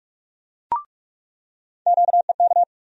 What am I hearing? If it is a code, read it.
Morse code, heard as a keyed tone, about 41 wpm, 700 Hz. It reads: QEK
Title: USA Call Sign Postfixes - 3 Letters 45wpm